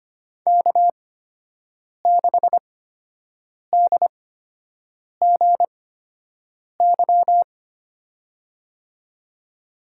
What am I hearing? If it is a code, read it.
Morse code, heard as a keyed tone, about 25 wpm, spaced as for 7 wpm, 700 Hz. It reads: K6DGY
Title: USA Call Signs - 8x Character Spacing - 25wpm